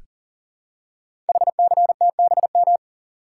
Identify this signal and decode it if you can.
Morse code, heard as a keyed tone, about 42 wpm, 690 Hz. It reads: HCTBM
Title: Code Groups - 5 Letters - 40wpm